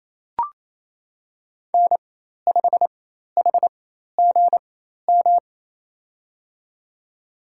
Morse code, heard as a keyed tone, about 28 wpm, spaced as for 13 wpm, 700 Hz. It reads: N5HGM
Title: USA Call Signs - 4x Character Spacing - 28wpm